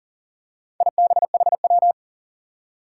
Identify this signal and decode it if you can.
Morse code, heard as a keyed tone, about 38 wpm, 700 Hz. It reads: IBHW
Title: Four Challenging Letter-Number Combinations 40wpm